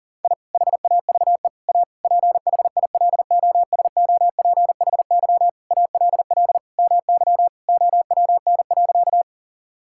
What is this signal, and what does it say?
Morse code, heard as a keyed tone, about 40 wpm, 700 Hz. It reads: I HAVE A PHILOSOPHY ALL MY OWN.
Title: Sentences from Top 1000 Words - Review Part 6 - 40wpm